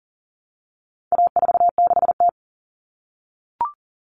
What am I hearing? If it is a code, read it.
Morse code, heard as a keyed tone, about 37 wpm, 700 Hz. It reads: A46T
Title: Common Cut Numbers - 4-Digit 40wpm